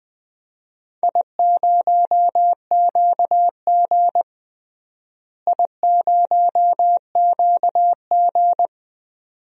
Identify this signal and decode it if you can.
Morse code, heard as a keyed tone, about 20 wpm, 700 Hz. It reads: I0QG I0QG
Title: DXCC-100 Call Signs - 4 Characters - Repeated 2x - 20wpm